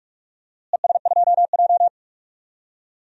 Morse code, heard as a keyed tone, about 47 wpm, 700 Hz. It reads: ES2J
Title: DXCC-100 Call Signs - 4 Characters - Repeated 2x - 45wpm